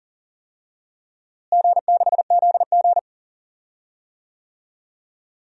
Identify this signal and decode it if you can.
Morse code, heard as a keyed tone, about 40 wpm, 700 Hz. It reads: G6ZG
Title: DXCC-100 Call Signs - 4 Characters - Repeated 3x - 40wpm